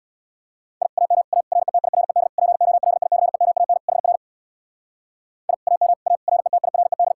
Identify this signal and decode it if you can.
Morse code, heard as a keyed tone, about 64 wpm, 700 Hz. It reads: I AM A RETIRED POLICEMAN HR I AM A RETIRED